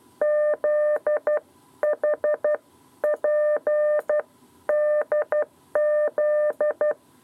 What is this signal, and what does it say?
Morse code, heard as a keyed tone, about 12 wpm, spaced as for 10 wpm, 600 Hz. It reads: ZHPDZ
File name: 7b lezione cw 2